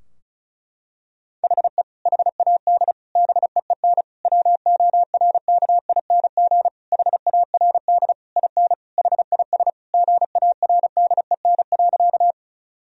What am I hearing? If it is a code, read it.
Morse code, heard as a keyed tone, about 34 wpm, 700 Hz. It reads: HE HAD BEEN WORKING HARD IN HIS GARDEN.
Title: Sentences from Top 1000 Words - Review Part 4 - 35wpm